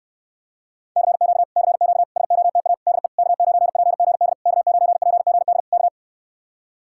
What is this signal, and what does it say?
Morse code, heard as a keyed tone, about 69 wpm, 700 Hz. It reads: CQ CQ SOTA DE K9PGD K9PGD K